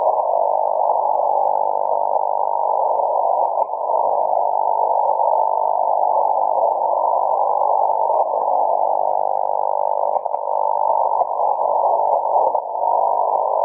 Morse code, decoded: none